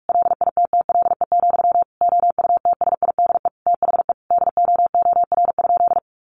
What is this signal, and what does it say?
Morse code, heard as a keyed tone, about 44 wpm, 700 Hz. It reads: LITTLE, OUTSIDE THE DOOR?
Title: Sentences from Top 900 Words 45wpm